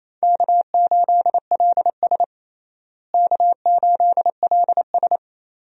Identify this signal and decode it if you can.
Morse code, heard as a keyed tone, about 28 wpm, 700 Hz. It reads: K8LS K8LS
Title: USA Call Signs - 4 Characters - Repeated 2x - 28wpm